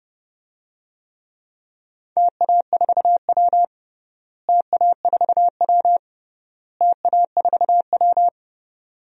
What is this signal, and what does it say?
Morse code, heard as a keyed tone, about 30 wpm, 700 Hz. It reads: TA4W TA4W TA4W
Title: DXCC-100 Call Signs - 4 Characters - Repeated 3x - 30wpm